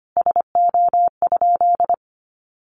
Morse code, heard as a keyed tone, about 25 wpm, 700 Hz. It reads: SO?
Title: Sentences from Top 1200 Words 25wpm